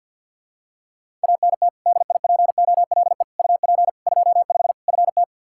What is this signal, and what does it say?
Morse code, heard as a keyed tone, about 49 wpm, 700 Hz. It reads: ANT DIPOLE UP 25 FT
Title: K1USN SST QSO Practice 50wpm